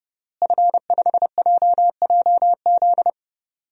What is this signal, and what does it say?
Morse code, heard as a keyed tone, about 30 wpm, 700 Hz. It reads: F5JJZ
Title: DXCC-100 Call Signs - 5 Characters - Repeated 3x - 30wpm